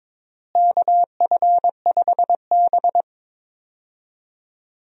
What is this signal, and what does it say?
Morse code, heard as a keyed tone, about 22 wpm, 700 Hz. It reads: KF5B
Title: USA Call Signs - 4 Characters - Repeated 2x - 22wpm